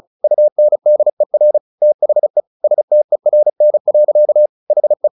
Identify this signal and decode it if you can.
Morse code, heard as a keyed tone, about 34 wpm, 600 Hz. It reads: UNDER THE STERN. HE
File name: the old man and the sea 2 35wpm